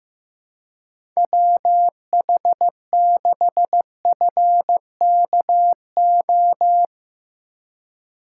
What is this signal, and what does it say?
Morse code, heard as a keyed tone, about 15 wpm, 700 Hz. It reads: WH6FKO
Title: USA Call Signs - 6 Characters - 2x Character Spacing - 15wpm